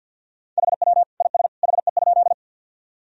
Morse code, heard as a keyed tone, about 51 wpm, 700 Hz. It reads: HW IS HE?
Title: Sentences from Top 100 Words and Abbreviations 50wpm